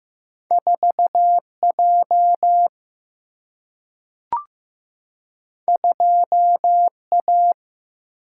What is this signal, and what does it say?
Morse code, heard as a keyed tone, about 15 wpm, 700 Hz. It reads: 4J 2A
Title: Single-Letter Single-Number 15wpm